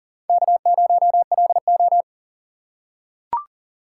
Morse code, heard as a keyed tone, about 42 wpm, 690 Hz. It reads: K0LO
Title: USA Call Signs - 4 Characters - 2x Character Spacing - 40wpm